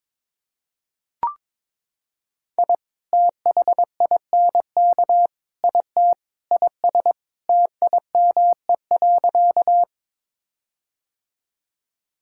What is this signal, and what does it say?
Morse code, heard as a keyed tone, about 22 wpm, 700 Hz. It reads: I THINK IT IS TIME.
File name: practice / 22wpm / Sentences with 4-Word Phrases - 22wpm